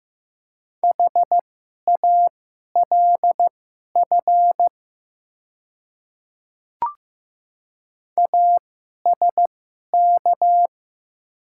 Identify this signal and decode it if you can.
Morse code, heard as a keyed tone, about 15 wpm, spaced as for 11 wpm, 700 Hz. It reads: HALF ASK
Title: 2x Character Spacing - Top 500 Words - 15wpm